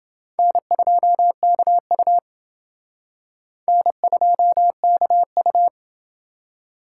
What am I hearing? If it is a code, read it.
Morse code, heard as a keyed tone, about 28 wpm, 700 Hz. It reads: N2KU N2KU